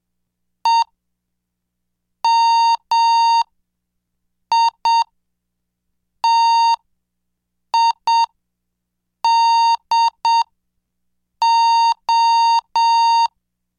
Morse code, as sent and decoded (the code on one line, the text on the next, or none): . -- .. - .. -.. ---
EMITIDO